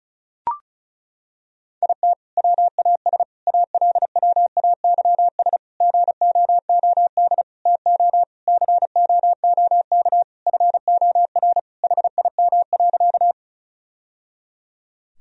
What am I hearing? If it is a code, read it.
Morse code, heard as a keyed tone, about 34 wpm, 700 Hz. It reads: IT WAS ALWAYS GOOD TO COOK FOR HIM.
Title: Sentences from Top 1000 Words - Review Part 4 - 35wpm